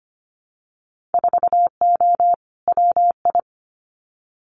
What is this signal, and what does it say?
Morse code, heard as a keyed tone, about 25 wpm, 700 Hz. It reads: 4O WI
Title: Winter Field Day Exchanges 25wpm